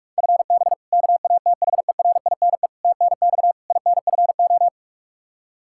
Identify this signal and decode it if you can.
Morse code, heard as a keyed tone, about 47 wpm, 700 Hz. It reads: FB KATHERINE TNX INFO
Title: Practice QSO - Exchange 3 - More Info - 45wpm